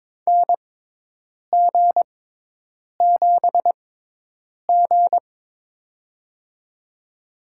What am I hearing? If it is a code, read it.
Morse code, heard as a keyed tone, about 22 wpm, spaced as for 8 wpm, 700 Hz. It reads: NG7G
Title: USA Call Signs - 4 Characters - 6x Character Spacing - 22wpm